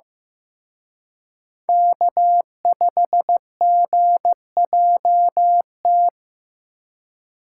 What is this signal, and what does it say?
Morse code, heard as a keyed tone, about 18 wpm, 700 Hz. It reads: K5GJT
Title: Speed Racing - USA Call Signs 15wpm